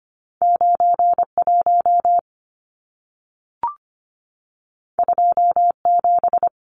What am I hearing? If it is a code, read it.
Morse code, heard as a keyed tone, about 25 wpm, 700 Hz. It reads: 91 27
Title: Numbers - 2-Digit 25wpm